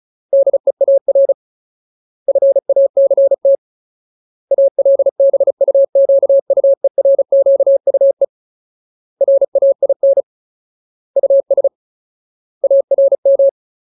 Morse code, heard as a keyed tone, about 35 wpm, 550 Hz. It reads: DEAR FACT ALBUQUERQUE RAIN US ARM
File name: Ultimate Word List / 35WPM Ultimate